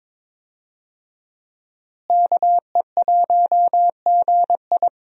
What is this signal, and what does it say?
Morse code, heard as a keyed tone, about 22 wpm, 700 Hz. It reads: KE1GI
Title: USA Call Signs - 5 Characters - Repeated 2x - 22wpm